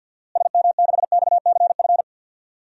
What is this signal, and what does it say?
Morse code, heard as a keyed tone, about 49 wpm, 700 Hz. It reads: SM6XCF
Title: DXCC-100 Call Signs - 6 Characters - Repeated 3x - 50wpm